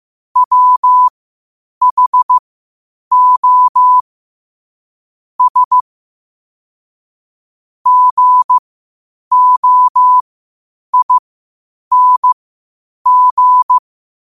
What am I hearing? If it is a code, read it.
Morse code, heard as a keyed tone, about 15 wpm, spaced as for 8 wpm, 1000 Hz. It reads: WHO S GOING
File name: qotdfiles / 8wpm